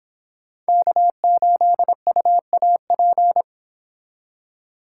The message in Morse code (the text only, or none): K8UAP